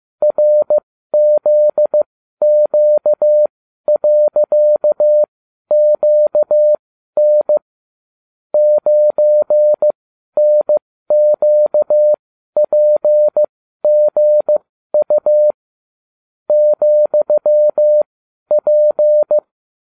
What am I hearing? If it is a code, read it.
Morse code, heard as a keyed tone, about 15 wpm, spaced as for 12 wpm, 600 Hz. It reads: RZQ.QN 9NQPGU ,P